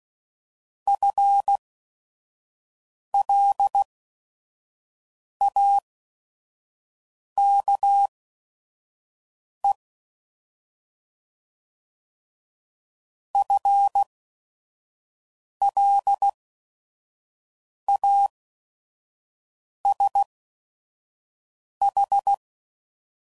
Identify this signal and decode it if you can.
Morse code, heard as a keyed tone, about 16 wpm, spaced as for 5 wpm, 800 Hz. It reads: FLAKE FLASH
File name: Prefixes / By First Letter / Fs